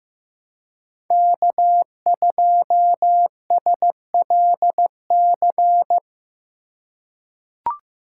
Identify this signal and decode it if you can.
Morse code, heard as a keyed tone, about 15 wpm, 700 Hz. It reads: K2SLC
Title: USA Call Signs - 5 Characters - 3x Character Spacing - 15wpm